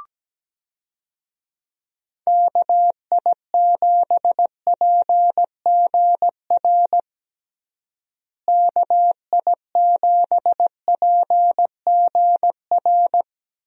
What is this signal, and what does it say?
Morse code, heard as a keyed tone, about 17 wpm, 700 Hz. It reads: KI7PGR KI7PGR
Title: USA Call Signs - 6 Characters - Repeated 3x - 17wpm